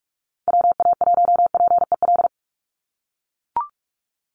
Morse code, heard as a keyed tone, about 44 wpm, 700 Hz. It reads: WA1PEL
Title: USA Call Signs 45wpm